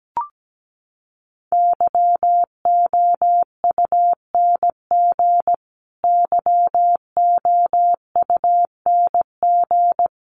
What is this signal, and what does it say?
Morse code, heard as a keyed tone, about 17 wpm, 700 Hz. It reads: YOUNG YOUNG
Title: Top 500 Words Repeated 17wpm